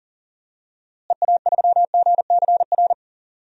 Morse code, heard as a keyed tone, about 39 wpm, 700 Hz. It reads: EA3GCR
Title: DXCC-100 Call Signs - 6 Characters - Repeated 3x - 40wpm